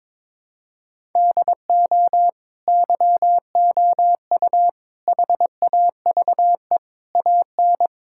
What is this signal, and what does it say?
Morse code, heard as a keyed tone, about 22 wpm, 700 Hz. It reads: DO YOU HAVE AN
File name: Sentences with 9-Word Phrases - 22wpm